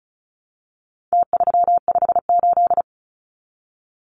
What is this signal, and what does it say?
Morse code, heard as a keyed tone, about 34 wpm, 700 Hz. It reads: T358
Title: Numbers - Contest Style - 4-Digit 35wpm